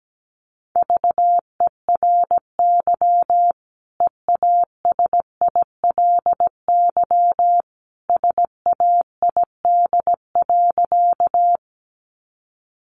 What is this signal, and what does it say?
Morse code, heard as a keyed tone, about 17 wpm, 700 Hz. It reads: VERY EASILY SAID.
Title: Sentences from Top 900 Words 17wpm